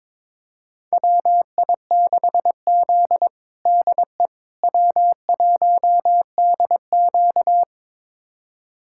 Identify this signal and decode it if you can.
Morse code, heard as a keyed tone, about 22 wpm, 700 Hz. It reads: WI6Z DE W1DQ
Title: USA Callsign DE Callsign - 4 Characters - 22wpm